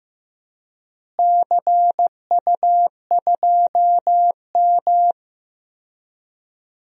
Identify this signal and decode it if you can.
Morse code, heard as a keyed tone, about 15 wpm, 700 Hz. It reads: CU2M
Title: DXCC-100 Call Signs - 4 Characters - Repeated 2x - 15wpm